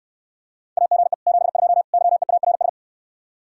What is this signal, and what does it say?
Morse code, heard as a keyed tone, about 69 wpm, 700 Hz. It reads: AGE 72 YEARS